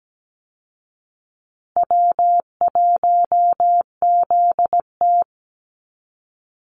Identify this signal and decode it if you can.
Morse code, heard as a keyed tone, about 17 wpm, 700 Hz. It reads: W1ZT
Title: Easy Call Signs Repeated 2x - 17wpm